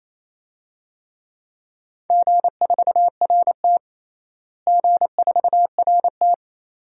Morse code, heard as a keyed tone, about 28 wpm, 700 Hz. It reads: G4RT G4RT
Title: DXCC-100 Call Signs - 4 Characters - Repeated 2x - 28wpm